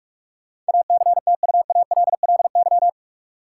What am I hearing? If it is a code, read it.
Morse code, heard as a keyed tone, about 46 wpm, 700 Hz. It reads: ACTUALLY
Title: Sentences from Top 700 Words - New Word Drill - 45wpm